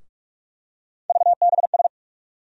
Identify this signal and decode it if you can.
Morse code, heard as a keyed tone, about 46 wpm, 700 Hz. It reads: VBS
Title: USA Call Sign Postfixes - 3 Letters 45wpm